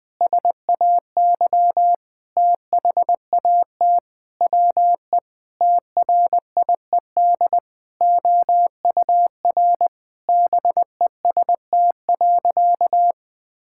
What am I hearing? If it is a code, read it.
Morse code, heard as a keyed tone, about 20 wpm, 700 Hz. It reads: SAY THAT WE TRIED OUR BEST.
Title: Sentences with 7-Word Phrases - 20wpm